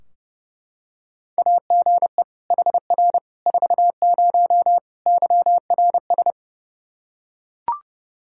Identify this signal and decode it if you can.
Morse code, heard as a keyed tone, about 30 wpm, 700 Hz. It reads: AGE HR 40 YRS
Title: Practice QSO - Exchange 3 - More Info - 30wpm